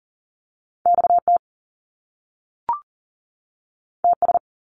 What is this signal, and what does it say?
Morse code, heard as a keyed tone, about 39 wpm, 700 Hz. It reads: XT TS
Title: Top Two Letter Combinations in Words 40wpm